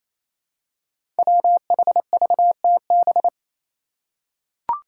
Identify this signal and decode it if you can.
Morse code, heard as a keyed tone, about 28 wpm, 700 Hz. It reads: WHVTB
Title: Five Challenging Letter-Number Combinations 28wpm